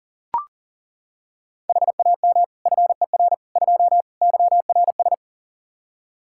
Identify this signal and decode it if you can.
Morse code, heard as a keyed tone, about 37 wpm, 700 Hz. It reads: HAM FER 2 YRS
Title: Practice QSO - Exchange 3 - More Info - 40wpm